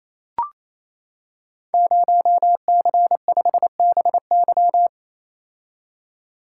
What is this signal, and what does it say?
Morse code, heard as a keyed tone, about 28 wpm, 700 Hz. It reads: 0C5BY
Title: Code Groups - 5 Letter:Numbers - 28wpm